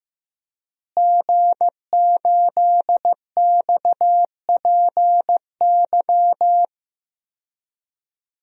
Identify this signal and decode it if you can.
Morse code, heard as a keyed tone, about 15 wpm, 700 Hz. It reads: G8XPY
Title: DXCC-100 Call Signs - 5 Characters - Repeated 3x - 15wpm